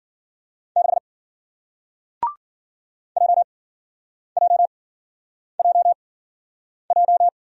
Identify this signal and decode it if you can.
Morse code, heard as a keyed tone, about 53 wpm, spaced as for 9 wpm, 700 Hz. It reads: 6 JJJJ